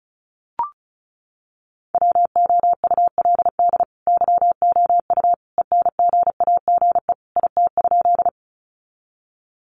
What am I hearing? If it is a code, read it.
Morse code, heard as a keyed tone, about 34 wpm, 700 Hz. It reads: WOULD YOU ENGAGE IT?
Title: Sentences from Top 1500 Words 35wpm